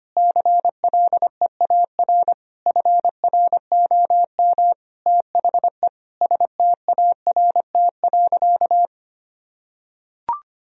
Morse code, heard as a keyed tone, about 25 wpm, 700 Hz. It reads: CLEAR FROM THE START.
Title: Sentences from Top 1000 Words - Review Part 1 - 25wpm